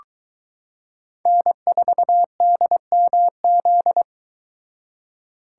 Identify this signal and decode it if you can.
Morse code, heard as a keyed tone, about 22 wpm, 700 Hz. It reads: N4DMZ